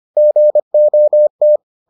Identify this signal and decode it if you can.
Morse code, heard as a keyed tone, about 25 wpm, 600 Hz. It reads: GOT